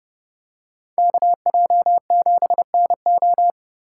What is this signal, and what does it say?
Morse code, heard as a keyed tone, about 29 wpm, 700 Hz. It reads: KJ7NO